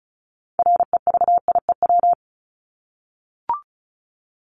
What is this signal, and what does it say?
Morse code, heard as a keyed tone, about 34 wpm, 700 Hz. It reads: REVIEW
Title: Top 751-1000 Words 35wpm